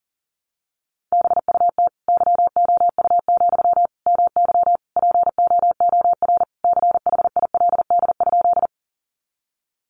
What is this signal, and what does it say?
Morse code, heard as a keyed tone, about 40 wpm, 700 Hz. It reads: BUT YOU, MY POOR CHILD?
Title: Sentences from Top 1000 Words 40wpm